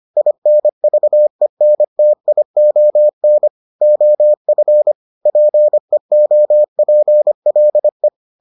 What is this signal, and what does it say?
Morse code, heard as a keyed tone, about 25 wpm, 600 Hz. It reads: INVENTION OF PEOPLE